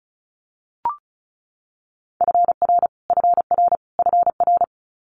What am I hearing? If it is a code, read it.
Morse code, heard as a keyed tone, about 36 wpm, 700 Hz. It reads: FR FR FR
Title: Multi-letter Phonemes Repeated 3x 35wpm